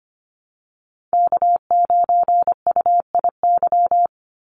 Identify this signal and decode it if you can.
Morse code, heard as a keyed tone, about 25 wpm, 700 Hz. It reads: K9UIY
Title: Call Signs Repeated 2x - 25wpm